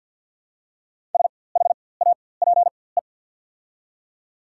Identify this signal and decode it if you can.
Morse code, heard as a keyed tone, about 49 wpm, spaced as for 23 wpm, 690 Hz. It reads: SHAPE
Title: 4x Character Spacing - Top 500 Words - 50wpm